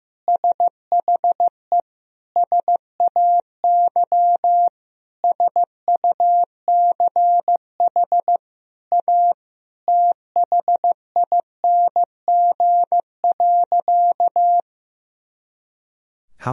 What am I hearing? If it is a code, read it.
Morse code, heard as a keyed tone, about 15 wpm, 700 Hz. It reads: SHE SAY SUCH A THING.
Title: Sentences from Top 200 Words and Abbreviations 15wpm